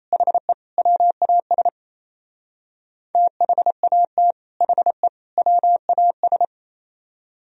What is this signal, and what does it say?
Morse code, heard as a keyed tone, about 33 wpm, 700 Hz. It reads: HE WAS THAT HE WAS
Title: Speed-Racing - Sets of 3 Words 28wpm